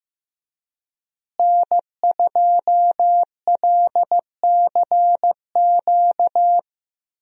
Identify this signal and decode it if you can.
Morse code, heard as a keyed tone, about 16 wpm, 700 Hz. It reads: N2LCQ
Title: USA Call Signs - 5 Characters - Repeated 2x - 15wpm